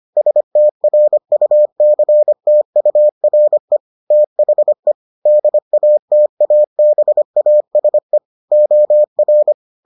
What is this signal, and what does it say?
Morse code, heard as a keyed tone, about 25 wpm, 600 Hz. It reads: STRUCTURE THE DATABASE OR